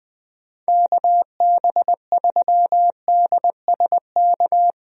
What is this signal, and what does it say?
Morse code, heard as a keyed tone, about 19 wpm, 700 Hz. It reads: KB3DSK
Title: USA Call Signs - 6 Characters - Repeated 2x - 20wpm